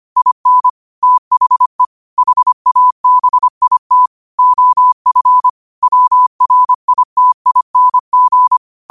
Morse code, heard as a keyed tone, about 25 wpm, 1000 Hz. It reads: IN THE HABIT OF WRITING